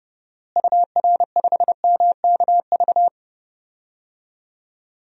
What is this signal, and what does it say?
Morse code, heard as a keyed tone, about 30 wpm, 700 Hz. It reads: UR5MKV
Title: DXCC-100 Call Signs - 6 Characters - Repeated 2x - 30wpm